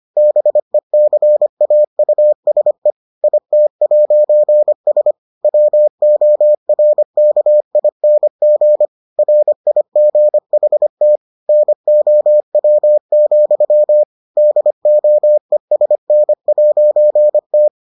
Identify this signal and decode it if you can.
Morse code, heard as a keyed tone, about 25 wpm, 600 Hz. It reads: BECAUSE IT'S WORKING RIGHT NOW, DOESN'T